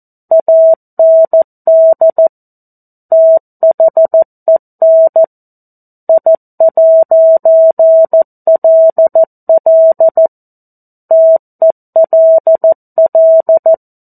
Morse code, heard as a keyed tone, about 14 wpm, 650 Hz. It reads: AND THEN I'LL TELL
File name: 02-Chapter-02 14 WPM